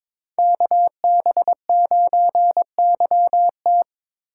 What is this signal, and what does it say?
Morse code, heard as a keyed tone, about 27 wpm, 700 Hz. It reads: KB9YT
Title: Speed Racing - USA 2x2 Call Signs 22wpm